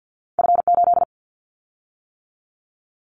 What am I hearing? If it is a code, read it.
Morse code, heard as a keyed tone, about 50 wpm, 700 Hz. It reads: F8
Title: Callsign Prefixes for Top 25 DXCC - 50wpm